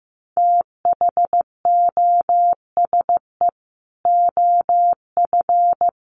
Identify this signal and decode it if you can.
Morse code, heard as a keyed tone, about 15 wpm, 700 Hz. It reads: THOSE OF
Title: Sets of 5 Words Encore2 15wpm